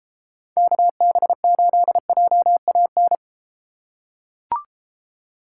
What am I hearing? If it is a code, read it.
Morse code, heard as a keyed tone, about 32 wpm, 700 Hz. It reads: KB8JAN